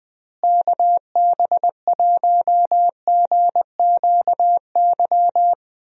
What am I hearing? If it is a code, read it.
Morse code, heard as a keyed tone, about 21 wpm, 700 Hz. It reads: KB1GQY
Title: USA Call Signs - 6 Characters - Repeated 3x - 20wpm